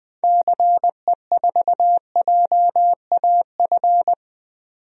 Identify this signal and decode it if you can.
Morse code, heard as a keyed tone, about 21 wpm, 700 Hz. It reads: CE4JAF
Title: DXCC-100 Call Signs - 6 Characters - 20wpm